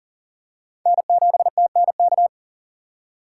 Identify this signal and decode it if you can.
Morse code, heard as a keyed tone, about 40 wpm, 700 Hz. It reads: N7TNK
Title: USA Call Signs - 5 Characters - 2x Character Spacing - 40wpm